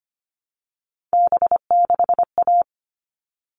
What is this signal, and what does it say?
Morse code, heard as a keyed tone, about 25 wpm, 700 Hz. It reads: B6A
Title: All Cut Numbers - 3-Digit 25wpm